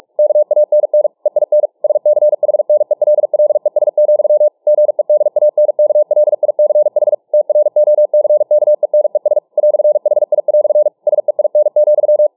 Morse code, heard as a keyed tone, about 48 wpm, 600 Hz. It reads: KANN EIN SCHNELLES, GEDANKLICH TROCKENES "HI" SEIN,